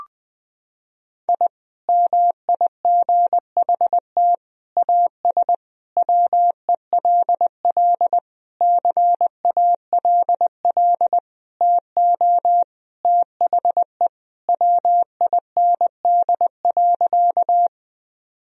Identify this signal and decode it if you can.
Morse code, heard as a keyed tone, about 20 wpm, 700 Hz. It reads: I MIGHT AS WELL CALL TO THE WIND.
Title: Sentences from Top 1000 Words - Review Part 2 - 20wpm